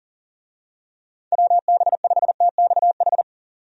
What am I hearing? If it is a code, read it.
Morse code, heard as a keyed tone, about 40 wpm, 700 Hz. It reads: WB5TXH
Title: USA Call Signs - 6 Characters - 40wpm